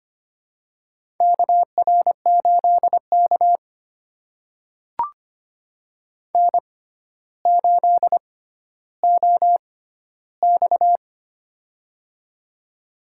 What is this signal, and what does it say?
Morse code, heard as a keyed tone, about 25 wpm, spaced as for 13 wpm, 700 Hz. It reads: KR8K N8OX
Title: USA Call Signs - 6x Character Spacing - 25wpm